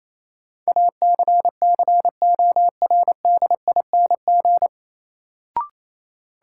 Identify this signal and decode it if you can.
Morse code, heard as a keyed tone, about 28 wpm, 700 Hz. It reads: ACCORDING
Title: Sentences from Top 700 Words - New Word Drill - 28wpm